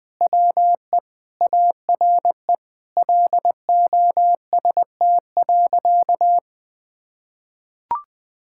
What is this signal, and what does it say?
Morse code, heard as a keyed tone, about 19 wpm, 700 Hz. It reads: WE ARE LOST.
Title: Sentences with 7-Word Phrases - 20wpm